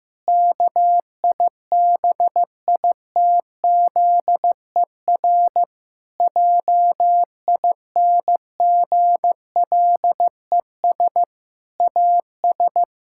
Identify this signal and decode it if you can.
Morse code, heard as a keyed tone, about 15 wpm, 700 Hz. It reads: KIBITZER JINGLES AS
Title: Pangram Challenge v2 - 15wpm